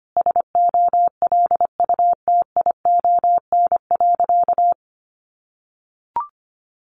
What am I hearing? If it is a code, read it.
Morse code, heard as a keyed tone, about 25 wpm, 700 Hz. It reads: SOLUTION.
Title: Sentences from Top 1300 Words 25wpm